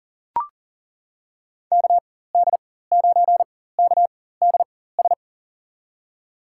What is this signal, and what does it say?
Morse code, heard as a keyed tone, about 40 wpm, spaced as for 19 wpm, 700 Hz. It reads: KD9KDS
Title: USA Call Signs - 6 Characters - 4x Character Spacing - 40wpm